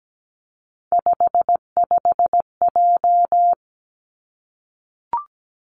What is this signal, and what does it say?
Morse code, heard as a keyed tone, about 17 wpm, 700 Hz. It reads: 55J
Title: Three Challenging Letter-Number Combinations 17wpm